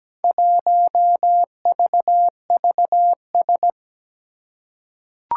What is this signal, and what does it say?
Morse code, heard as a keyed tone, about 17 wpm, 700 Hz. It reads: 1VVS
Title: Four Challenging Letter-Number Combinations 17wpm